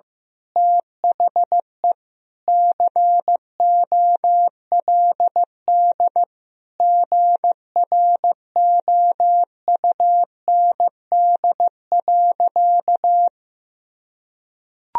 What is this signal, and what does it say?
Morse code, heard as a keyed tone, about 15 wpm, 700 Hz. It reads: THE COLD GROUND.
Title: Sentences from Top 1000 Words - Review Part 3 - 15wpm